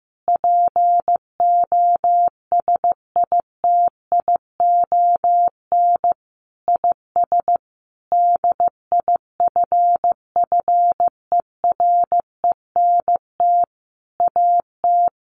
Morse code, heard as a keyed tone, about 15 wpm, 700 Hz. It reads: POSITION IS DIFFERENT AT
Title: Sentences from Top 1400 Words 15wpm